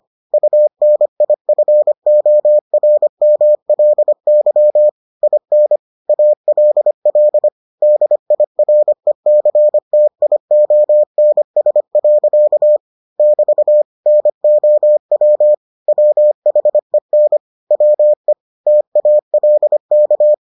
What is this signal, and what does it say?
Morse code, heard as a keyed tone, about 25 wpm, 600 Hz. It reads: UNIFORMLY IN ALL DIRECTIONS. <BT> NOW WHEN WE TALK